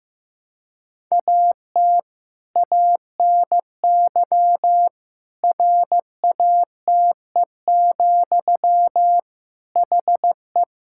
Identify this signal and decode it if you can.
Morse code, heard as a keyed tone, about 15 wpm, 700 Hz. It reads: AT ANY RATE, HE
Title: Sentences from Top 1000 Words - Review Part 5 - 15wpm